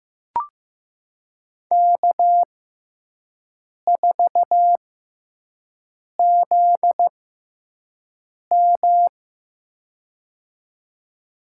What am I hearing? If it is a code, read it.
Morse code, heard as a keyed tone, about 15 wpm, spaced as for 5 wpm, 700 Hz. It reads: K4ZM
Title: USA Call Signs - 6x Character Spacing - 15wpm